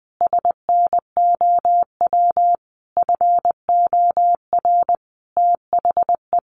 Morse code, heard as a keyed tone, about 19 wpm, 700 Hz. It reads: SNOW FOR THE
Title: Easy Phrases 20wpm